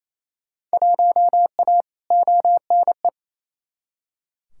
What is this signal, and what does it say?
Morse code, heard as a keyed tone, about 28 wpm, 700 Hz. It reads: 1A ONE
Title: Field Day Exchanges Encore - 28wpm